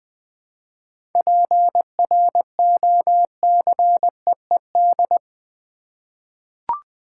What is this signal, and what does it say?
Morse code, heard as a keyed tone, about 20 wpm, 700 Hz. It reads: PROCEED
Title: 7 Letter Words - First 3 Letters Given - 20wpm